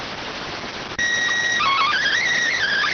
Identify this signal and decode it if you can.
SSTV (slow-scan television), PD 290.